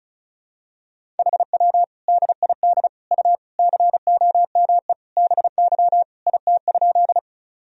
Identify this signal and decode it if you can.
Morse code, heard as a keyed tone, about 34 wpm, 700 Hz. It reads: HW DID U COME BY IT?
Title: Sentences from Top 100 Words and Abbreviations 35wpm